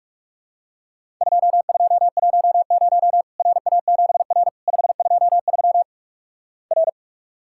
Morse code, heard as a keyed tone, about 48 wpm, 700 Hz. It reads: 2210 RA7R 523 R
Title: Worked All Europe DX Contest - QTC Exchange - 45wpm